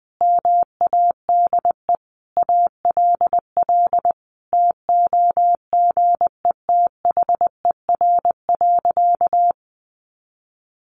Morse code, heard as a keyed tone, about 20 wpm, 700 Hz. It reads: MADE ALL TOGETHER.
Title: Sentences from Top 1100 Words 20wpm